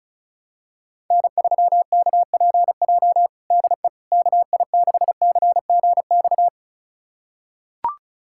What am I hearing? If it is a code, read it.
Morse code, heard as a keyed tone, about 34 wpm, 700 Hz. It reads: N3KPJ DE KI6CGX
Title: USA Callsign DE Callsign 35wpm